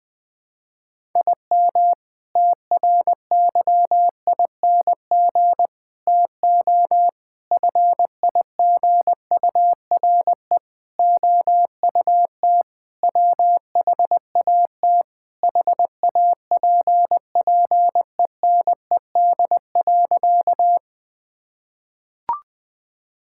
Morse code, heard as a keyed tone, about 19 wpm, 700 Hz. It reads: IM TRYING TO FIGURE OUT WHAT HAPPENED.
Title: Sentences with 5-Word Phrases - 20wpm